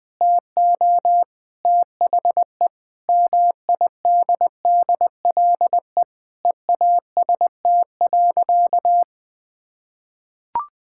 Morse code, heard as a keyed tone, about 20 wpm, 700 Hz. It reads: TO THE MIDDLE EAST.
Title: Sentences with 3-Word Phrases - 20wpm